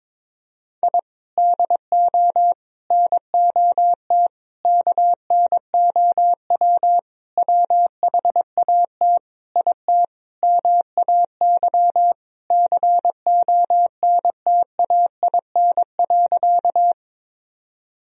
Morse code, heard as a keyed tone, about 22 wpm, 700 Hz. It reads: I DO NOT KNOW WHAT IT MAY CONTAIN.
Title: Sentences from Top 1000 Words - Review Part 2 - 22wpm